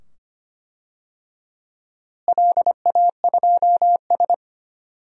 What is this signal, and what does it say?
Morse code, heard as a keyed tone, about 25 wpm, 700 Hz. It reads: LA2S
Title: DXCC-100 Call Signs - 4 Characters - Repeated 2x - 25wpm